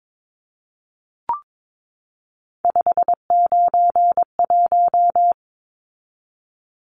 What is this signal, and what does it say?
Morse code, heard as a keyed tone, about 22 wpm, 700 Hz. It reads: 591
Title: Numbers - 3-Digit 22wpm